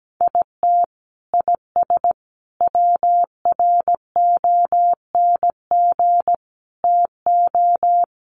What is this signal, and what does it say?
Morse code, heard as a keyed tone, about 17 wpm, 700 Hz. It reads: IT IS WRONG TO
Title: Sentences from Top 1300 Words 17wpm